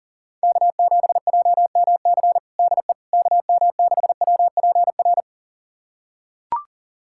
Morse code, heard as a keyed tone, about 40 wpm, 700 Hz. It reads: K7JMC DE KM6WPR
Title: USA Callsign DE Callsign 40wpm